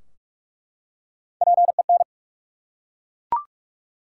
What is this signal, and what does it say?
Morse code, heard as a keyed tone, about 44 wpm, 700 Hz. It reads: PEN
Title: Three Letter Words - 45wpm